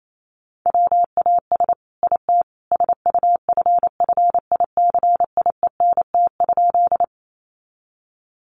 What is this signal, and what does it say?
Morse code, heard as a keyed tone, about 28 wpm, 700 Hz. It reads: WAS IT SUFFICIENT?
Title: Sentences from Top 1400 Words 28wpm